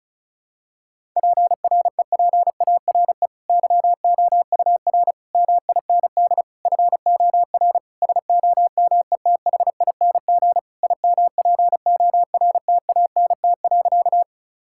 Morse code, heard as a keyed tone, about 36 wpm, 700 Hz. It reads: PREPARE YOUR MIND FOR SOMETHING IMPORTANT.
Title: Sentences from Top 1000 Words - Review Part 6 - 35wpm